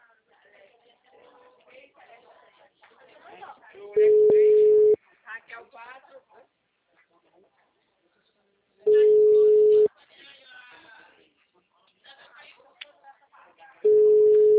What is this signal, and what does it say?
Ringback tone (Europe, 425 Hz), with speech.